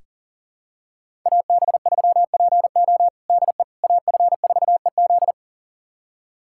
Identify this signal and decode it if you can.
Morse code, heard as a keyed tone, about 37 wpm, 700 Hz. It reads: AB3PO DE AF4EZ
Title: USA Callsign DE Callsign - 5 Characters - 40wpm